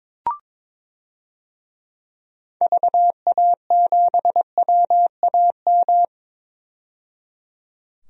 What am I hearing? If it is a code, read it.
Morse code, heard as a keyed tone, about 22 wpm, 700 Hz. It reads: VA7WAM
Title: DXCC-100 Call Signs - 6 Characters - 22wpm